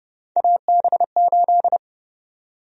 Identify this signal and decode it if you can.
Morse code, heard as a keyed tone, about 31 wpm, 700 Hz. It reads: AB8
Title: USA Call Sign Prefixes - 30wpm